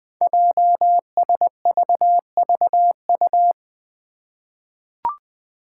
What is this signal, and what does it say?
Morse code, heard as a keyed tone, about 20 wpm, 700 Hz. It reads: JSVVU
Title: Five Challenging Letter-Number Combinations 20wpm